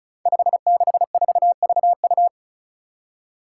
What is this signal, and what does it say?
Morse code, heard as a keyed tone, about 34 wpm, 700 Hz. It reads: 564VU
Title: Five Challenging Letter-Number Combinations 35wpm